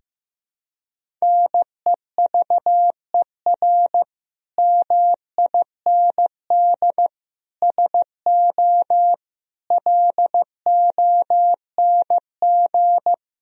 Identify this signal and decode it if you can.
Morse code, heard as a keyed tone, about 15 wpm, 700 Hz. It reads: NEVER MIND SO LONG
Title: Sentences from Top 1000 Words - Review Part 1 - 15wpm